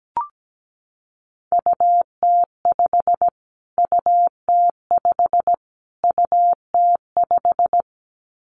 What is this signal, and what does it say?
Morse code, heard as a keyed tone, about 17 wpm, 700 Hz. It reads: UT5 UT5 UT5
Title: Callsign Prefixes for Top 25 DXCC - Repeated 3x - 17wpm